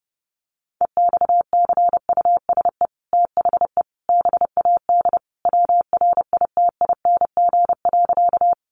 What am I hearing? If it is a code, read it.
Morse code, heard as a keyed tone, about 30 wpm, 700 Hz. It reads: EXCUSE THE BAD WRITING.
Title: Sentences from Top 800 Words 30wpm